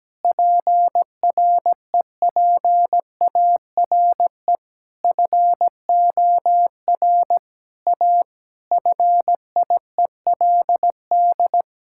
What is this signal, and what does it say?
Morse code, heard as a keyed tone, about 17 wpm, 700 Hz. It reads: PREPARE FOR A FIELD
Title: Sentences from Top 1000 Words - Review Part 6 - 17wpm